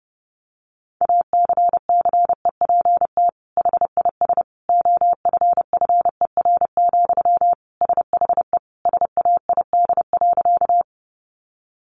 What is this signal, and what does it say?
Morse code, heard as a keyed tone, about 30 wpm, 700 Hz. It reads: ACCEPT HIS OFFER, SHE SAID.
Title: Sentences from Top 1300 Words 30wpm